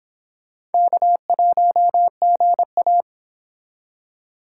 Morse code, heard as a keyed tone, about 26 wpm, 700 Hz. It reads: K1GA
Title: Speed Racing - USA 1x2 Call Signs 22wpm